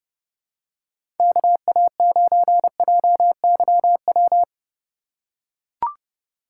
Morse code, heard as a keyed tone, about 32 wpm, 700 Hz. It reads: KA9JYW
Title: USA Call Signs - 6 Characters - 4x Character Spacing - 30wpm